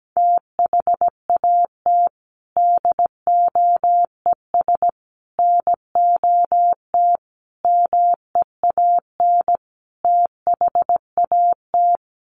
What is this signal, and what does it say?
Morse code, heard as a keyed tone, about 17 wpm, 700 Hz. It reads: THAT DOES NOT MEAN THAT
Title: Sets of 5 Words Encore 17wpm